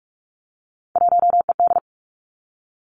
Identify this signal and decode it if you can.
Morse code, heard as a keyed tone, about 46 wpm, 700 Hz. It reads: 1ED
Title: Three Challenging Letter-Number Combinations 45wpm